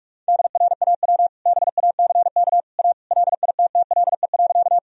Morse code, heard as a keyed tone, about 45 wpm, 700 Hz. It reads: DRAW BACK A LITTLE.
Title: Sentences from Top 1000 Words - Review Part 1 - 45wpm